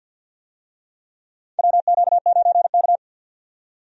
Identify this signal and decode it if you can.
Morse code, heard as a keyed tone, about 48 wpm, 700 Hz. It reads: WQ9K
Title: USA Call Signs - 4 Characters - 50wpm